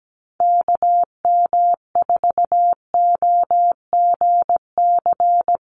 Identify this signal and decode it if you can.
Morse code, heard as a keyed tone, about 17 wpm, 700 Hz. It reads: KM4OGC
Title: USA Call Signs Repeated 3x 17wpm